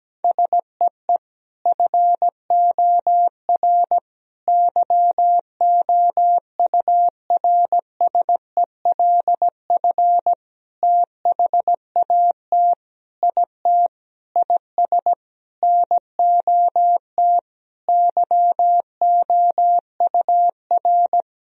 Morse code, heard as a keyed tone, about 17 wpm, 700 Hz. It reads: SEE FOR YOURSELF THAT IT IS NOT YOUR
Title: Sentences from Top 1000 Words - Review Part 1 - 17wpm